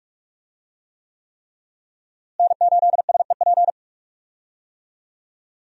Morse code, heard as a keyed tone, about 46 wpm, 700 Hz. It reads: N8SEP
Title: USA Call Signs - 5 Characters - Repeated 3x - 45wpm